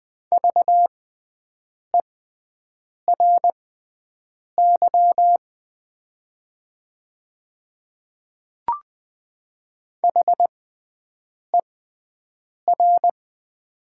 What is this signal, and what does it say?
Morse code, heard as a keyed tone, about 20 wpm, spaced as for 7 wpm, 700 Hz. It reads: VERY HER
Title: 6x Character Spacing - Top 200 Words - 20wpm